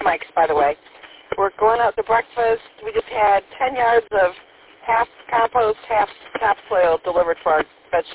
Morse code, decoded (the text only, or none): none